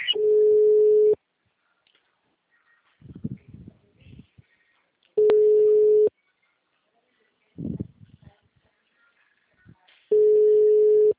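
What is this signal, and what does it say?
Ringback tone (Europe, 425 Hz).
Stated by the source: Outgoing telephone call from the Presidente Prudente - SP Mauricio system.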